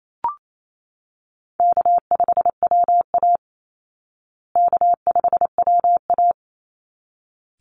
Morse code, heard as a keyed tone, about 28 wpm, 700 Hz. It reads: K5WA K5WA
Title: Easy Call Signs Repeated 2x - 28wpm